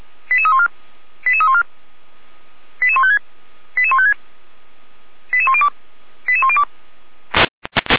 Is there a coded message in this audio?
Yes: five-tone selective call (ZVEI) from 0.3 to 6.6 s, heard as quick runs of five tones, tones of about 70 ms. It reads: 88314 88314 88156 88156 88102 88102